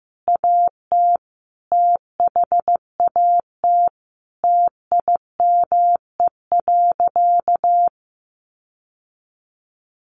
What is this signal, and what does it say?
Morse code, heard as a keyed tone, about 15 wpm, 700 Hz. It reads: AT THAT TIME.
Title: Sentences from Top 1300 Words 15wpm